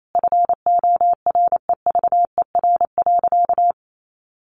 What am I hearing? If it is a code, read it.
Morse code, heard as a keyed tone, about 28 wpm, 700 Hz. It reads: FOREVER.
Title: A Sound Of Thunder 28wpm